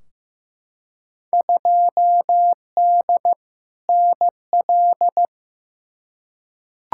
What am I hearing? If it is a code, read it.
Morse code, heard as a keyed tone, about 15 wpm, 700 Hz. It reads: 2D NL
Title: Field Day Exchanges Encore - 15wpm